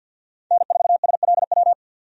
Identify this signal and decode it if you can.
Morse code, heard as a keyed tone, about 52 wpm, 700 Hz. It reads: N4SLW
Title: USA Call Signs - 5 Characters - 50wpm